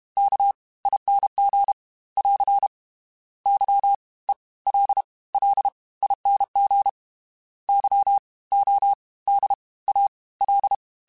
Morse code, heard as a keyed tone, about 32 wpm, 800 Hz. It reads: K ING <AR> Y E L L ING Y O D A L
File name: Special INGs / Special INGs-mp3s / Fast ing @30